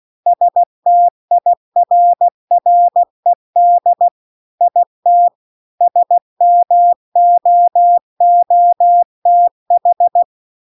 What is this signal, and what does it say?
Morse code, heard as a keyed tone, about 16 wpm, 700 Hz. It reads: STIRRED IT SMOOTH
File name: CW-The-Big-Twoheaded-River-2-16WPM